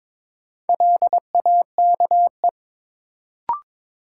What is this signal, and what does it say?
Morse code, heard as a keyed tone, about 22 wpm, 700 Hz. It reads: LAKE
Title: Sentences from Top 1100 Words - New Word Drill Alt - 22wpm